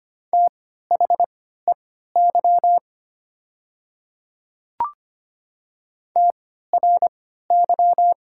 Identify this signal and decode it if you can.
Morse code, heard as a keyed tone, about 25 wpm, spaced as for 14 wpm, 700 Hz. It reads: THEY TRY
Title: 3x Character Spacing - Top 200 Words - 25wpm